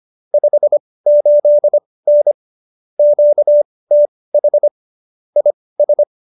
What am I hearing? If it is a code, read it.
Morse code, heard as a keyed tone, about 25 wpm, spaced as for 18 wpm, 590 Hz. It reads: 58N QTH IS
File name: qso215b 18